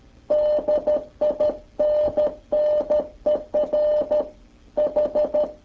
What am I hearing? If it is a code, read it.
Morse code, heard as a keyed tone, about 12 wpm, 600 Hz. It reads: DINNER H